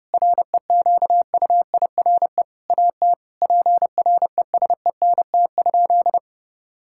Morse code, heard as a keyed tone, about 30 wpm, 700 Hz. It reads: REQUIRE AT PRESENT?
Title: Sentences from Top 1000 Words - Review Part 6 - 30wpm